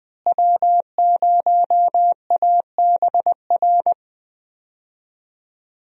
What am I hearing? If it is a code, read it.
Morse code, heard as a keyed tone, about 20 wpm, 700 Hz. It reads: W0ABR
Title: USA Call Signs - 5 Characters - Repeated 3x - 20wpm